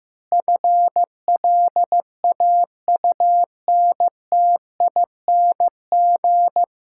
Morse code, heard as a keyed tone, about 16 wpm, 700 Hz. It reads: FLAUNTING
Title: Pangram Challenge v2 - 15wpm